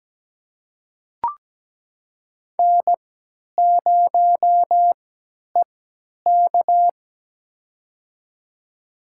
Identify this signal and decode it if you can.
Morse code, heard as a keyed tone, about 17 wpm, spaced as for 10 wpm, 700 Hz. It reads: N0EK
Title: USA Call Signs - 4 Characters - 3x Character Spacing - 17wpm